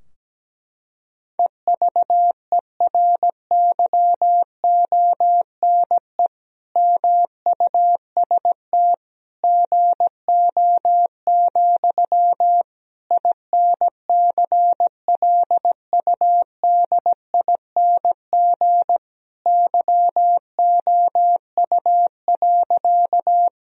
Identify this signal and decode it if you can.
Morse code, heard as a keyed tone, about 17 wpm, 700 Hz. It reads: EVERYONE MUST GO, INCLUDING YOU.
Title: Sentences from Top 1000 Words - Review Part 4 - 17wpm